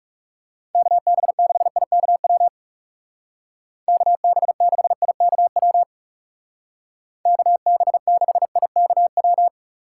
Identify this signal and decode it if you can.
Morse code, heard as a keyed tone, about 37 wpm, 700 Hz. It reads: KB6IKW KB6IKW KB6IKW
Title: Speed Racing - USA 2x3 Call Signs 30wpm